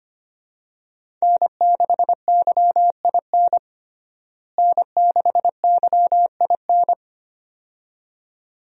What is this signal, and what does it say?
Morse code, heard as a keyed tone, about 25 wpm, 700 Hz. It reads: N6YIN N6YIN
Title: USA Call Signs - 5 Characters - Repeated 2x - 25wpm